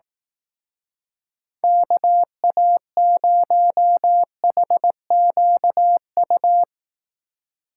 Morse code, heard as a keyed tone, about 18 wpm, 700 Hz. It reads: KA0HQU
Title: Speed Racing - USA 2x3 Call Signs 15wpm